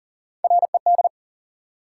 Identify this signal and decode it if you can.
Morse code, heard as a keyed tone, about 39 wpm, 700 Hz. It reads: RED
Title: Sentences from Top 700 Words - New Word Drill - 40wpm